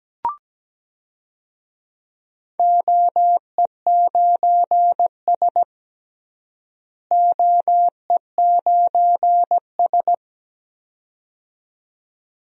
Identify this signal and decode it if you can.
Morse code, heard as a keyed tone, about 17 wpm, 700 Hz. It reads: OE9S OE9S
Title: DXCC-100 Call Signs - 4 Characters - Repeated 2x - 17wpm